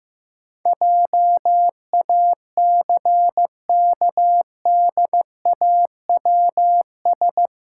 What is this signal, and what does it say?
Morse code, heard as a keyed tone, about 16 wpm, 700 Hz. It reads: JACKDAWS
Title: Pangram Challenge v2 - 15wpm